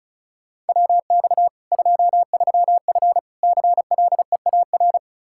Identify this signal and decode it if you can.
Morse code, heard as a keyed tone, about 36 wpm, 700 Hz. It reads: WX 23F CLEAR
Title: K1USN SST QSO Practice 35wpm